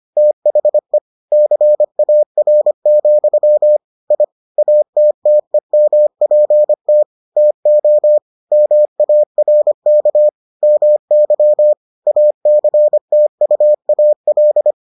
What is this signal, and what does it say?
Morse code, heard as a keyed tone, about 25 wpm, 600 Hz. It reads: THE CAR, I ATTEMPT TO MARK MY ACTUAL